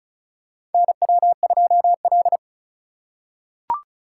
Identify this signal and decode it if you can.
Morse code, heard as a keyed tone, about 36 wpm, 700 Hz. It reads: NW2L
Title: USA Call Signs - 4 Characters - 4x Character Spacing - 35wpm